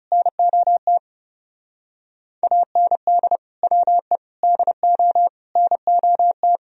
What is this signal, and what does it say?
Morse code, heard as a keyed tone, about 36 wpm, 700 Hz. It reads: NOT AND WE DO NOT
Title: Speed-Racing - Sets of 4 Words 30wpm